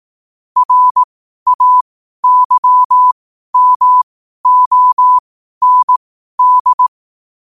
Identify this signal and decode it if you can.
Morse code, heard as a keyed tone, about 18 wpm, spaced as for 12 wpm, 1000 Hz. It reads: RAYMOND
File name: qotdfiles / 12wpm